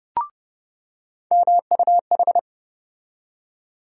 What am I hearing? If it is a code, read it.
Morse code, heard as a keyed tone, about 30 wpm, 700 Hz. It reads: MUH
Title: USA Call Sign Postfixes - 3 Letters 30wpm